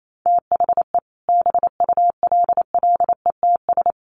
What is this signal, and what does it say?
Morse code, heard as a keyed tone, about 28 wpm, 700 Hz. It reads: THE BULLETS
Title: A Sound Of Thunder 28wpm